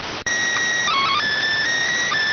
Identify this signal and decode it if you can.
SSTV (slow-scan television), PD 290.